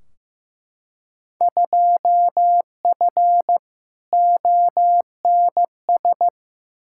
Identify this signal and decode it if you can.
Morse code, heard as a keyed tone, about 15 wpm, 700 Hz. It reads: 2F ONS
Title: Field Day Exchanges - 15wpm